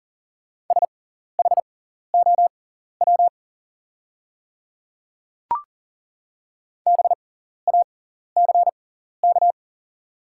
Morse code, heard as a keyed tone, about 40 wpm, spaced as for 14 wpm, 700 Hz. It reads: SHOW BACK
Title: 6x Character Spacing - Top 200 Words - 40wpm